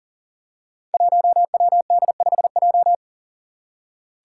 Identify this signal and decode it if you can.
Morse code, heard as a keyed tone, about 40 wpm, 700 Hz. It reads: 1WD5J
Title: Five Challenging Letter-Number Combinations 40wpm